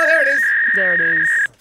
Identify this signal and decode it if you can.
Fax machine handshake, with speech.